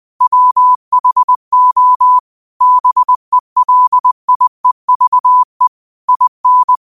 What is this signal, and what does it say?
Morse code, heard as a keyed tone, about 20 wpm, 1000 Hz. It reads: WHO BELIEVE IN
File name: qotdfiles / 20wpm